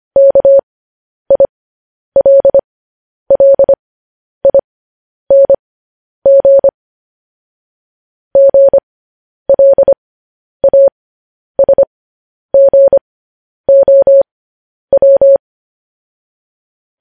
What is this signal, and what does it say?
Morse code, heard as a keyed tone, about 25 wpm, spaced as for 10 wpm, 550 Hz. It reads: KILLING GLASGOW
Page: news-headlines-in-morse-code-at-10-wpm